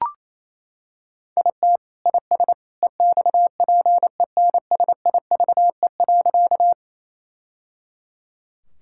Morse code, heard as a keyed tone, about 27 wpm, 700 Hz. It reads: IT IS EXPENSIVE.
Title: Sentences from Top 1000 Words - Review Part 6 - 28wpm